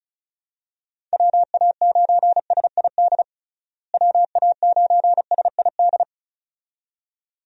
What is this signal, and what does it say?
Morse code, heard as a keyed tone, about 36 wpm, 700 Hz. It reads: WA9SID WA9SID
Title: USA Call Signs - 6 Characters - Repeated 2x - 35wpm